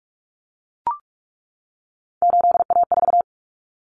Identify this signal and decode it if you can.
Morse code, heard as a keyed tone, about 44 wpm, 700 Hz. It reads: 8A4
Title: Common Cut Numbers - 3-Digit 45wpm